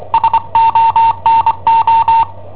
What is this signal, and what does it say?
Morse code, heard as a keyed tone, about 25 wpm, 910 Hz. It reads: SONO
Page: Cassetta esercizi preparata da I7OHP per IW5ELP